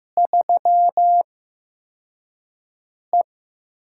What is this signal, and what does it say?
Morse code, heard as a keyed tone, about 15 wpm, spaced as for 4 wpm, 700 Hz. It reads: 3E
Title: USA Call Signs - 4 Characters - 8x Character Spacing - 15wpm